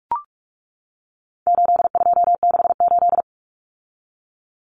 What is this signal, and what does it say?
Morse code, heard as a keyed tone, about 44 wpm, 700 Hz. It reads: 8268
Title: Numbers - 4-Digit 45wpm